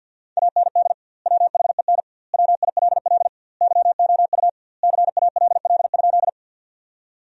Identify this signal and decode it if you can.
Morse code, heard as a keyed tone, about 52 wpm, 700 Hz. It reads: AND WHEN WILL YOU CALL?
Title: 2x Word-Spacing - Sentences from Top 100 Words - 50wpm